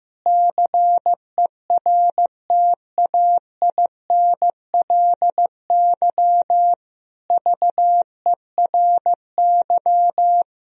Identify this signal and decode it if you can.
Morse code, heard as a keyed tone, about 15 wpm, 700 Hz. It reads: CERTAINLY VERY
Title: Sentences from Top 1000 Words - Review Part 2 - 15wpm